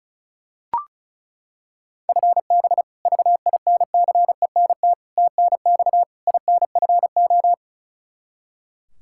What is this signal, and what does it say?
Morse code, heard as a keyed tone, about 36 wpm, 700 Hz. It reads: FB VINCENT TNX INFO
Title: Practice QSO - Exchange 3 - More Info - 35wpm